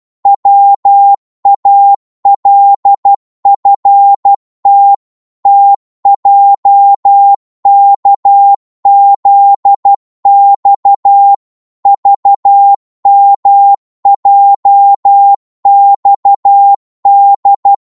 Morse code, heard as a keyed tone, about 12 wpm, 800 Hz. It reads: WALFT TJKZX VMJXD
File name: Morse / ltr-12wpm